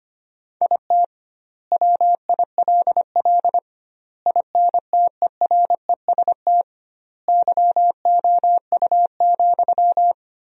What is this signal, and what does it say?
Morse code, heard as a keyed tone, about 25 wpm, 700 Hz. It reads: IT WILL INTEREST YOU,
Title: 2x Word-Spacing - Sentences from Top 400 Words - 25wpm